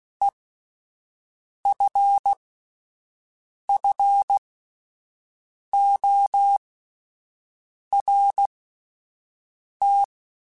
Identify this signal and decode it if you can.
Morse code, heard as a keyed tone, about 16 wpm, spaced as for 5 wpm, 800 Hz. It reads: EFFORT
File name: Alphabetical / Alphabetical mp3s / Es @5